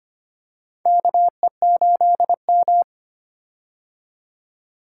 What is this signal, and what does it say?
Morse code, heard as a keyed tone, about 25 wpm, 700 Hz. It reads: KE8M
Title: K1USN SST QSO Practice - 25wpm